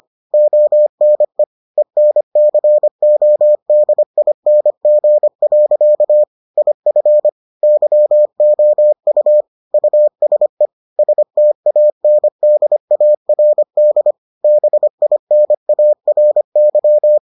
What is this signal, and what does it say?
Morse code, heard as a keyed tone, about 25 wpm, 600 Hz. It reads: ONE ENCODING. IF YOU USE STANDARD BINARY